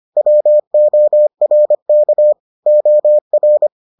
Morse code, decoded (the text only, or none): WORK OR